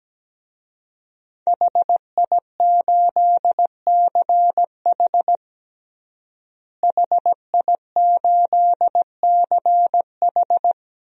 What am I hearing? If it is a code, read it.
Morse code, heard as a keyed tone, about 17 wpm, 700 Hz. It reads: HI8CH HI8CH
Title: DXCC-100 Call Signs - 5 Characters - Repeated 3x - 17wpm